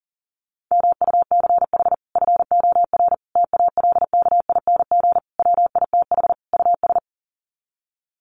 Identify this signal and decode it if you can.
Morse code, heard as a keyed tone, about 40 wpm, 700 Hz. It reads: MUCH FOR TALKING WITH US
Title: Sets of 5 Words Encore 40wpm